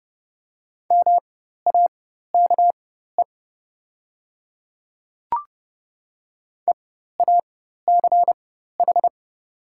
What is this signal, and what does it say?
Morse code, heard as a keyed tone, about 30 wpm, spaced as for 14 wpm, 700 Hz. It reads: MAKE EACH
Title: 4x Character Spacing - Top 100 Words - 30wpm